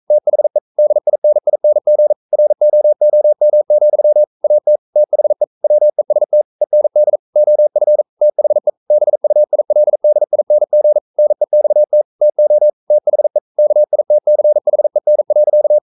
Morse code, heard as a keyed tone, about 42 wpm, 600 Hz. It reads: THE DINING ROOM, AT THE WEST END OF THE BUILDING NEXT TO THE KITCHEN.
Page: Romanen Never af Ken Follet på engelsk.